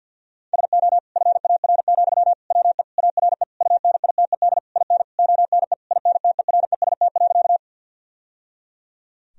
Morse code, heard as a keyed tone, about 50 wpm, 700 Hz. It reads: SO FAR, WE ARE UNITED IN ONE INTEREST.
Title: Sentences from Top 1000 Words - Review Part 4 - 50wpm